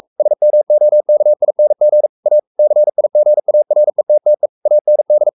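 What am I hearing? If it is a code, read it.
Morse code, heard as a keyed tone, about 44 wpm, 600 Hz. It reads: SMOKING A CIGARETTE AND
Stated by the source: Morse code at 43 wpm sending Ken Follett's novel Never in English, from CW med Gnister.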